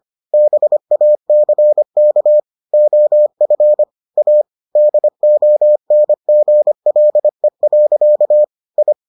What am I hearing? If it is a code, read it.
Morse code, heard as a keyed tone, about 25 wpm, 600 Hz. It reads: BACK OF A DONGLE. I